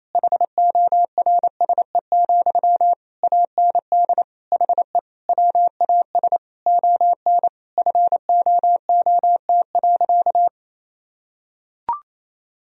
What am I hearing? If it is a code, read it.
Morse code, heard as a keyed tone, about 28 wpm, 700 Hz. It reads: HORSE, AND HE WAS ON FOOT.
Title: Sentences from Top 1000 Words - Review Part 3 - 28wpm